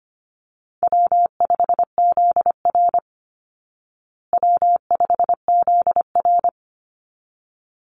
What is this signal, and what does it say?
Morse code, heard as a keyed tone, about 25 wpm, 700 Hz. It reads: W5ZR W5ZR
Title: Easy Call Signs Repeated 2x - 25wpm